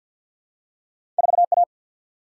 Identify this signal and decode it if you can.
Morse code, heard as a keyed tone, about 50 wpm, 700 Hz. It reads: 4A
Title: Single-Letter Single-Number 50wpm